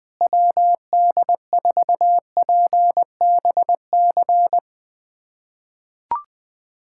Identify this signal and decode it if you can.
Morse code, heard as a keyed tone, about 20 wpm, 700 Hz. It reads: WD4PBC
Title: USA Callsign DE Callsign - 6 Characters - 20wpm